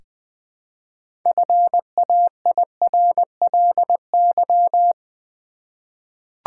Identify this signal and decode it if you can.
Morse code, heard as a keyed tone, about 20 wpm, 700 Hz. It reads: FAIRLY
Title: Sentences from Top 1100 Words - New Word Drill v2 - 20wpm